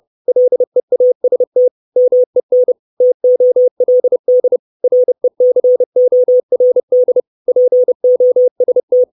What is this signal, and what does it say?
Morse code, heard as a keyed tone, about 30 wpm, 500 Hz. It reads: LEAST MEN TOLD RECORD POST
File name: 632 Words 30WPM